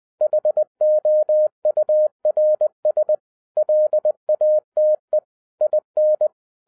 Morse code, heard as a keyed tone, about 20 wpm, 610 Hz. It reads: HOURS LATE IN
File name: 2011-07-mp3 / n33 THE 3.11 2 20wpm